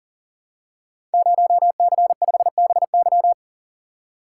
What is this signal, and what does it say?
Morse code, heard as a keyed tone, about 40 wpm, 700 Hz. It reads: 0C5BY
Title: Code Groups - 5 Letter:Numbers - 40wpm